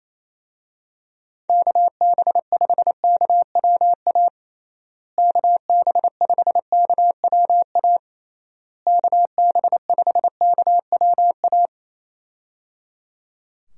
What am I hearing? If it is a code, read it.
Morse code, heard as a keyed tone, about 28 wpm, 700 Hz. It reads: KB5KWA KB5KWA KB5KWA
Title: USA Call Signs - 6 Characters - Repeated 3x - 28wpm